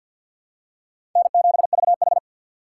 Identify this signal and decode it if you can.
Morse code, heard as a keyed tone, about 47 wpm, 700 Hz. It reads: N7VH
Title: USA Call Signs - 4 Characters - 3x Character Spacing - 50wpm